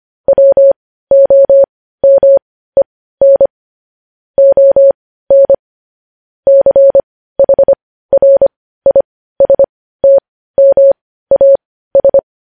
Morse code, heard as a keyed tone, about 25 wpm, spaced as for 15 wpm, 550 Hz. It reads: WOMEN ON CHRISTMAS
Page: news headlines in morse code at 15 wpm